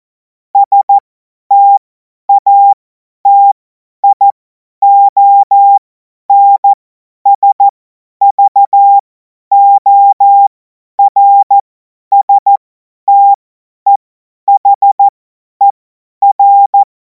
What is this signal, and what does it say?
Morse code, heard as a keyed tone, about 14 wpm, spaced as for 10 wpm, 800 Hz. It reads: STATIONSVORSTEHER
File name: Morsekurs / Witze / Witz 10. Februar 2026 50 F70